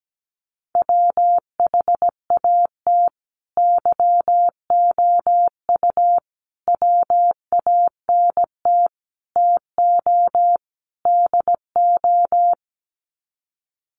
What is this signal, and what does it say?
Morse code, heard as a keyed tone, about 17 wpm, 700 Hz. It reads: WHAT YOU WANT TO DO
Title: Sets of 5 Words Encore 17wpm